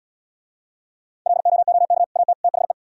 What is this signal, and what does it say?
Morse code, heard as a keyed tone, about 79 wpm, 700 Hz. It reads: LOOK AT THE